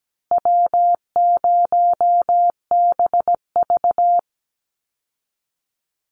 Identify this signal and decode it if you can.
Morse code, heard as a keyed tone, about 17 wpm, 700 Hz. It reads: W0BV
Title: Easy Call Signs Repeated 2x - 17wpm